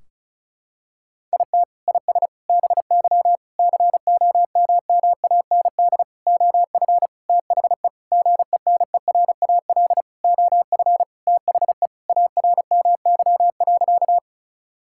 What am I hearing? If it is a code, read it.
Morse code, heard as a keyed tone, about 36 wpm, 700 Hz. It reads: IT IS BY COMMAND OF THE GENERAL OF THE ARMY.
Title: Sentences from Top 1600 Words 35wpm